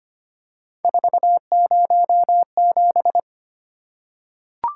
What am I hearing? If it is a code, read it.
Morse code, heard as a keyed tone, about 25 wpm, 700 Hz. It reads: 407
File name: Three Letter-Number Combinations - 25wpm